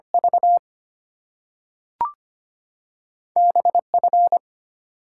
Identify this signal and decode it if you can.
Morse code, heard as a keyed tone, about 25 wpm, 700 Hz. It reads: V B F
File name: Two Letter-Number Combinations - 25wpm